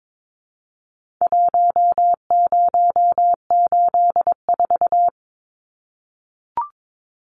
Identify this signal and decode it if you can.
Morse code, heard as a keyed tone, about 22 wpm, 700 Hz. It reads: 1084
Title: Numbers - Contest Style - 4-Digit 22wpm